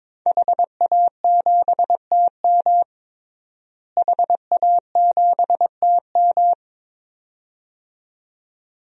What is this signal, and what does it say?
Morse code, heard as a keyed tone, about 23 wpm, 700 Hz. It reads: HA7TM HA7TM
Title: DXCC-100 Call Signs - 5 Characters - Repeated 2x - 22wpm